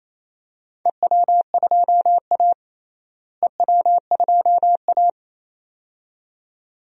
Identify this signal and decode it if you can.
Morse code, heard as a keyed tone, about 28 wpm, 700 Hz. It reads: EW2A EW2A
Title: DXCC-100 Call Signs - 4 Characters - Repeated 2x - 28wpm